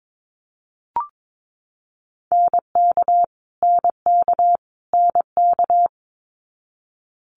Morse code, heard as a keyed tone, about 22 wpm, 700 Hz. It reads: NK NK NK
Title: Multi-letter Phonemes Repeated 3x 22wpm